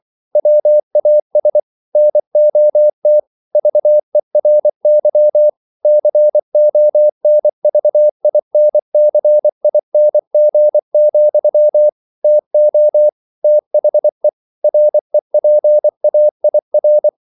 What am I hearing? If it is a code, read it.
Morse code, heard as a keyed tone, about 24 wpm, 600 Hz. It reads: WAS NOT VERY CONVINCING, TO THE REPAIR